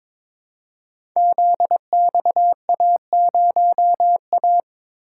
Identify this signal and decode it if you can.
Morse code, heard as a keyed tone, about 22 wpm, 700 Hz. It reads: ZXA0A
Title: Code Groups - 5 Letter:Numbers - 22wpm